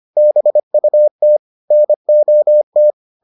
Morse code, heard as a keyed tone, about 25 wpm, 600 Hz. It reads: BUT NOT